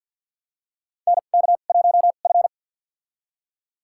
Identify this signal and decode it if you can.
Morse code, heard as a keyed tone, about 51 wpm, spaced as for 36 wpm, 700 Hz. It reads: NK1F
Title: USA Call Signs - 4 Characters - 2x Character Spacing - 50wpm